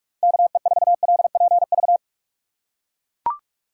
Morse code, heard as a keyed tone, about 42 wpm, 700 Hz. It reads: KE4LPV
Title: USA Call Signs - 6 Characters - 6x Character Spacing - 45wpm